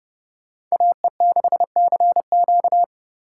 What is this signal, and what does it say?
Morse code, heard as a keyed tone, about 32 wpm, 700 Hz. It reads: AE6CQ